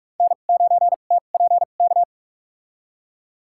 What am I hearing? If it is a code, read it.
Morse code, heard as a keyed tone, about 46 wpm, spaced as for 32 wpm, 700 Hz. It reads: N9TPK
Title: USA Call Signs - 2x Character Spacing - 45wpm